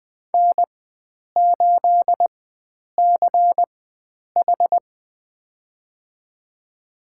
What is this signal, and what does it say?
Morse code, heard as a keyed tone, about 20 wpm, spaced as for 9 wpm, 700 Hz. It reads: N8CH
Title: USA Call Signs - 4 Characters - 4x Character Spacing - 20wpm